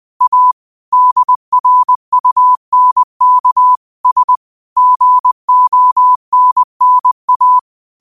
Morse code, heard as a keyed tone, about 20 wpm, 1000 Hz. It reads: A DRUNK S GONNA